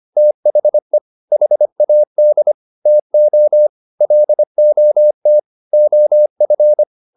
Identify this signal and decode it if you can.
Morse code, heard as a keyed tone, about 25 wpm, 600 Hz. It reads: THE HAD TO LOT OF